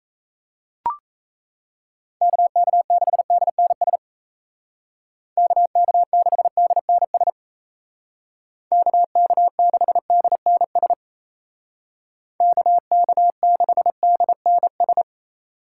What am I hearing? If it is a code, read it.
Morse code, heard as a keyed tone, about 34 wpm, 700 Hz. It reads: KK6DNS KK6DNS KK6DNS KK6DNS
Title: Speed Racing - USA 2x3 Call Signs 28wpm